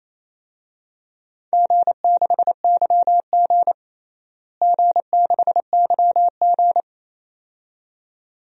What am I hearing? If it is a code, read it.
Morse code, heard as a keyed tone, about 28 wpm, 700 Hz. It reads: G6YG G6YG
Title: DXCC-100 Call Signs - 4 Characters - Repeated 2x - 28wpm